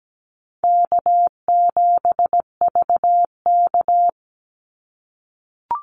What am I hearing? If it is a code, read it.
Morse code, heard as a keyed tone, about 17 wpm, 700 Hz. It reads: K7VK
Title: NA SOTA Activator Callsigns 17wpm